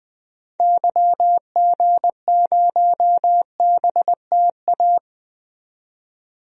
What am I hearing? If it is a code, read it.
Morse code, heard as a keyed tone, about 20 wpm, 700 Hz. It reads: YG0BTA
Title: DXCC-100 Call Signs - 6 Characters - 20wpm